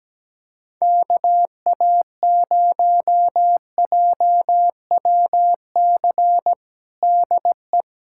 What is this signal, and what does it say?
Morse code, heard as a keyed tone, about 17 wpm, 700 Hz. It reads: KA0JWC DE
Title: USA Callsign DE Callsign - 6 Characters - 17wpm